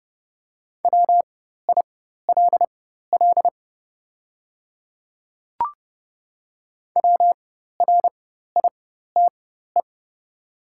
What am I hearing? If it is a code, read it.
Morse code, heard as a keyed tone, about 30 wpm, spaced as for 14 wpm, 700 Hz. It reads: WILL WRITE
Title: 4x Character Spacing - Top 100 Words - 30wpm